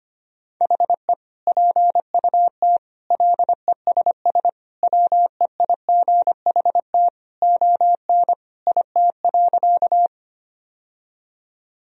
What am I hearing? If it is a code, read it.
Morse code, heard as a keyed tone, about 25 wpm, 700 Hz. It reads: HE PUT LESS WEIGHT ON IT.
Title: Sentences from Top 1000 Words - Review Part 3 - 25wpm